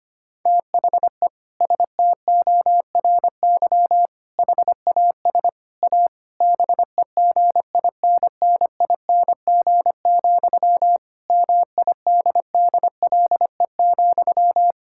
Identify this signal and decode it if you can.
Morse code, heard as a keyed tone, about 25 wpm, 700 Hz. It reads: THE STORY HAS A BEGINNING, MIDDLE,
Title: Sentences from Top 1000 Words - Review Part 6 - 25wpm